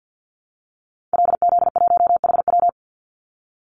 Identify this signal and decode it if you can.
Morse code, heard as a keyed tone, about 50 wpm, 700 Hz. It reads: LZ1HW
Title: Call Signs Repeated 2x - 50wpm